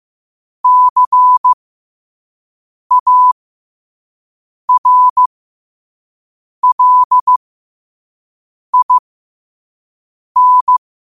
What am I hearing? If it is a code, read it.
Morse code, heard as a keyed tone, about 15 wpm, spaced as for 5 wpm, 1000 Hz. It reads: CARLIN